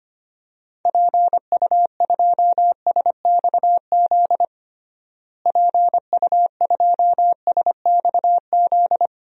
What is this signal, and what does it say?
Morse code, heard as a keyed tone, about 25 wpm, 700 Hz. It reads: PU2SXZ PU2SXZ
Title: DXCC-100 Call Signs - 6 Characters - Repeated 2x - 25wpm